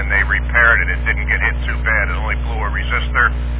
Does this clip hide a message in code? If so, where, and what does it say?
Yes: Emergency Alert System (SAME) tones from 0.3 to 3.4 s, heard as harsh buzzing bursts, with speech.